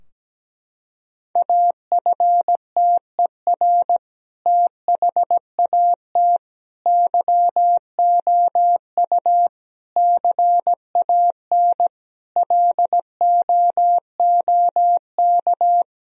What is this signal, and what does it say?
Morse code, heard as a keyed tone, about 17 wpm, 700 Hz. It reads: AFTER THAT YOU CAN LOOK
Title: Sentences from Top 1000 Words - Review Part 2 - 17wpm